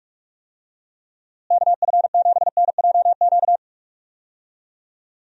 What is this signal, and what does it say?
Morse code, heard as a keyed tone, about 41 wpm, 700 Hz. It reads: KF7NJQ
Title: USA Call Signs - 6 Characters - Repeated 3x - 45wpm